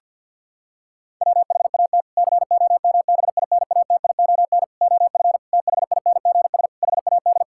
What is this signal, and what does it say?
Morse code, heard as a keyed tone, about 50 wpm, 700 Hz. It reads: WHAT COMBINATION OF THINGS HAD
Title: Sentences from Top 1000 Words - Review Part 6 - 50wpm